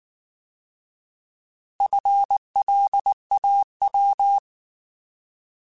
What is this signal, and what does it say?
Morse code, heard as a keyed tone, about 19 wpm, 800 Hz. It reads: FLAW
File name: Alphabetical / Alphabetical mp3s / Fs @18